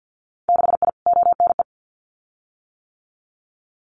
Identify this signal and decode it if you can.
Morse code, heard as a keyed tone, about 50 wpm, 700 Hz. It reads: 6I ONE
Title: Winter Field Day Exchanges Encore 50wpm